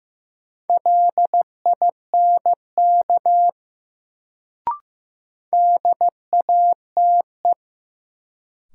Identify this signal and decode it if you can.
Morse code, heard as a keyed tone, about 15 wpm, 700 Hz. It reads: LINK DATE
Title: Sentences from Top 1000 Words - Review Part 7 - New Word Drill - 15wpm